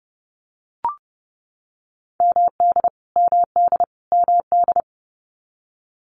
Morse code, heard as a keyed tone, about 30 wpm, 700 Hz. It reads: MD MD MD
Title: US States Repeated 3x 30wpm